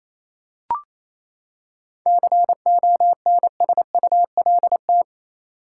Morse code, heard as a keyed tone, about 28 wpm, 700 Hz. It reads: CONSULT
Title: Seven Letter Words - 28wpm